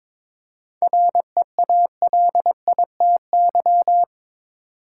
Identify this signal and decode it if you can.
Morse code, heard as a keyed tone, about 22 wpm, 700 Hz. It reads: REALITY
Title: Seven Letter Words - 22wpm